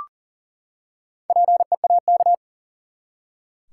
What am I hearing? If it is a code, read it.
Morse code, heard as a keyed tone, about 40 wpm, 700 Hz. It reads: PEAK
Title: Four Letter Words - 40wpm